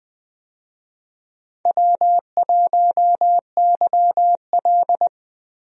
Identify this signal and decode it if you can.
Morse code, heard as a keyed tone, about 20 wpm, 700 Hz. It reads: W1YL
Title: USA Call Signs - 4 Characters - 20wpm